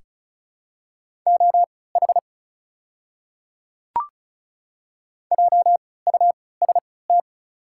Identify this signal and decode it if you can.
Morse code, heard as a keyed tone, about 36 wpm, spaced as for 20 wpm, 700 Hz. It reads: OH JUST
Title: 3x Character Spacing - Top 500 Words - 35wpm